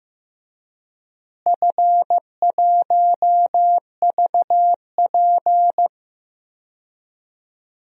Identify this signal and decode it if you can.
Morse code, heard as a keyed tone, about 15 wpm, 700 Hz. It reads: F1VP
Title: DXCC-100 Call Signs - 4 Characters - Repeated 2x - 15wpm